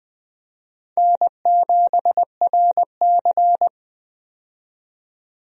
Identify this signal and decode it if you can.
Morse code, heard as a keyed tone, about 20 wpm, 700 Hz. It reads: N7RC
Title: USA Call Signs - 4 Characters - 20wpm